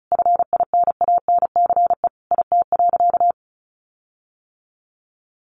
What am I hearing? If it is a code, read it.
Morse code, heard as a keyed tone, about 34 wpm, 700 Hz. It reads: FINANCE IT.
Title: Sentences from Top 1500 Words 35wpm